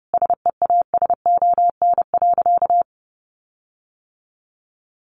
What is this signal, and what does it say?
Morse code, heard as a keyed tone, about 30 wpm, 700 Hz. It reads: SEASON.
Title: Sentences from Top 1400 Words 30wpm